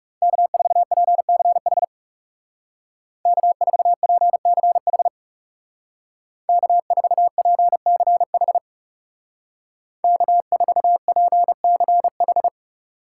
Morse code, heard as a keyed tone, about 35 wpm, 700 Hz. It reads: K4PCH K4PCH K4PCH K4PCH
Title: Speed Racing - USA 1x3 Call Signs 30wpm